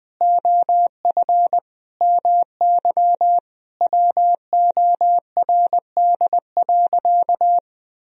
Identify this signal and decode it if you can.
Morse code, heard as a keyed tone, about 20 wpm, 700 Hz. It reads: OF MY WORD.
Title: Sentences from Top 1000 Words - Review Part 2 - 20wpm